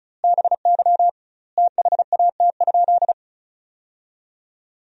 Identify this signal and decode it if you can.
Morse code, heard as a keyed tone, about 36 wpm, 700 Hz. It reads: BY THAT?
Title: 2x Word-Spacing - Sentences from Top 200 Words - 35wpm